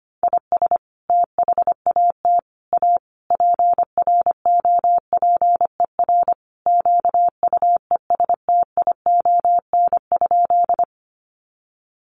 Morse code, heard as a keyed tone, about 25 wpm, 700 Hz. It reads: IS THAT A PROPER QUESTION?
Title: Sentences from Top 1100 Words 25wpm